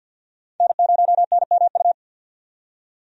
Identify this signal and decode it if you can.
Morse code, heard as a keyed tone, about 52 wpm, 700 Hz. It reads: N0NMU
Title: USA Call Signs - 5 Characters - 50wpm